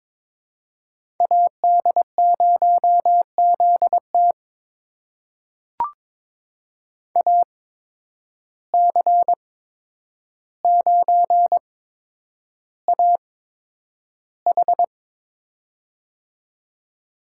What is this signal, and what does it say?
Morse code, heard as a keyed tone, about 22 wpm, spaced as for 9 wpm, 700 Hz. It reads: AD0ZT AC9AH
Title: USA Call Signs - 5 Characters - 8x Character Spacing - 22wpm